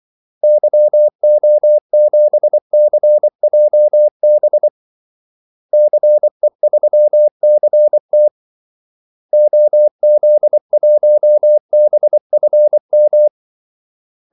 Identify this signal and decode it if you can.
Morse code, heard as a keyed tone, about 24 wpm, 600 Hz. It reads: YO7CJB CE3CT OZ1BFM
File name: Callsigns-24wpm